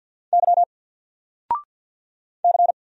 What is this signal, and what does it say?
Morse code, heard as a keyed tone, about 50 wpm, 700 Hz. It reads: Y C
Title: Single Letter-Number 50wpm